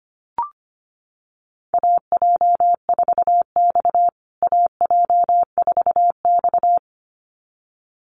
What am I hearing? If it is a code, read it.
Morse code, heard as a keyed tone, about 25 wpm, 700 Hz. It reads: AJ4X AJ4X
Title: USA Call Signs Repeated 2x 25wpm